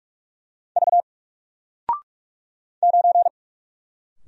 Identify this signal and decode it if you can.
Morse code, heard as a keyed tone, about 44 wpm, 700 Hz. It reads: V 9
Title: Single Letter-Number Alt 45wpm